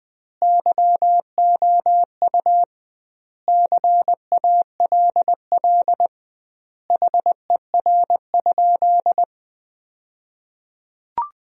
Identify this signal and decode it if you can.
Morse code, heard as a keyed tone, about 20 wpm, 700 Hz. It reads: YOU CALL HER?
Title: 2x Word-Spacing - Sentences from Top 100 Words - 20wpm